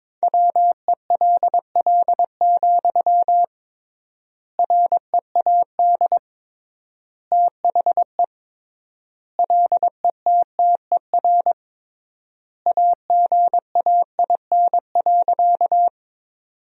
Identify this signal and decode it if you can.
Morse code, heard as a keyed tone, about 21 wpm, 700 Hz. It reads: WELL, READ THE LETTER AGAIN.
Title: 3x Word-Spacing - Sentences from Top 300 Words - 22wpm